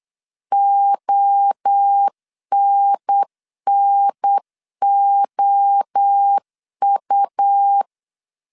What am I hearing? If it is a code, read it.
Morse code, heard as a keyed tone, about 8 wpm, 800 Hz. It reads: ONNOU
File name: CW / DET / 1 3 1500C S B 8wpm 4141